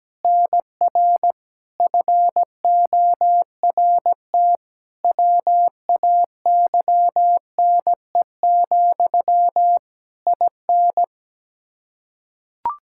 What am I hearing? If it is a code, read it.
Morse code, heard as a keyed tone, about 17 wpm, 700 Hz. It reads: NR FORT WAYNE, IN
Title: Practice QSO - Exchange 1 - Beginning - 17wpm